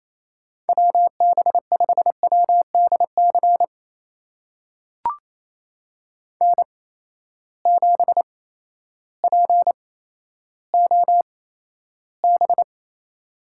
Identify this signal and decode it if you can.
Morse code, heard as a keyed tone, about 28 wpm, spaced as for 13 wpm, 700 Hz. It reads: WB5WDC N7POB
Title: USA Call Signs - 8x Character Spacing - 28wpm